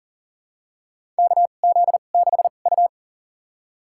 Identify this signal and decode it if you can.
Morse code, heard as a keyed tone, about 37 wpm, spaced as for 28 wpm, 700 Hz. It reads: KZ6U
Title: USA Call Signs - 4 Characters - 2x Character Spacing - 40wpm